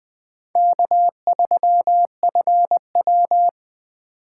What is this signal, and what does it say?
Morse code, heard as a keyed tone, about 20 wpm, 700 Hz. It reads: K3FW
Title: USA Call Signs - 4 Characters - 2x Character Spacing - 20wpm